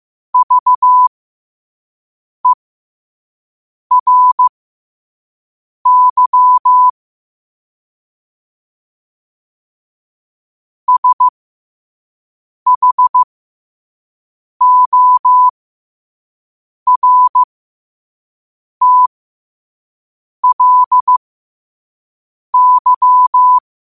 Morse code, heard as a keyed tone, about 15 wpm, spaced as for 5 wpm, 1000 Hz. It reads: VERY SHORTLY
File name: qotdfiles / 5wpm